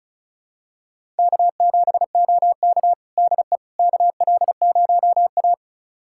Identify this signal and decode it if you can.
Morse code, heard as a keyed tone, about 34 wpm, 700 Hz. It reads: K7OK DE KL0A
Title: USA Callsign DE Callsign - 4 Characters - 35wpm